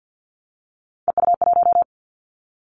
Morse code, heard as a keyed tone, about 50 wpm, 700 Hz. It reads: EU1
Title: Callsign Prefixes for Top 100 DXCC - 50wpm